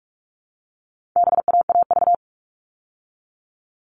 Morse code, heard as a keyed tone, about 44 wpm, 700 Hz. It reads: BAAV
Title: All Cut Numbers - 4-Digit 45wpm